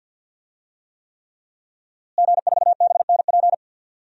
Morse code, heard as a keyed tone, about 46 wpm, 700 Hz. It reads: G4BNP
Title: DXCC-100 Call Signs - 5 Characters - Repeated 3x - 50wpm